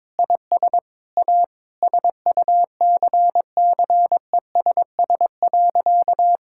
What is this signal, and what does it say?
Morse code, heard as a keyed tone, about 22 wpm, 700 Hz. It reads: IS A SUCCESS.
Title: Sentences from Top 1000 Words - Review Part 6 - 22wpm